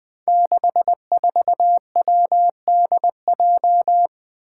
Morse code, heard as a keyed tone, about 20 wpm, 700 Hz. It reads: 64WDJ
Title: Five Challenging Letter-Number Combinations 20wpm